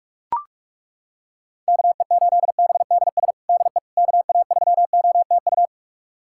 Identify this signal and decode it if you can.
Morse code, heard as a keyed tone, about 42 wpm, 700 Hz. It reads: KE8BDS DE KA3OTU
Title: USA Callsign DE Callsign 45wpm